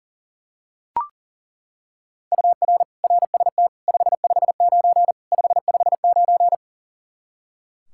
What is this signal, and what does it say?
Morse code, heard as a keyed tone, about 40 wpm, 700 Hz. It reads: UR RST 559 559
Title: Practice QSO - Exchange 1 - Beginning - 40wpm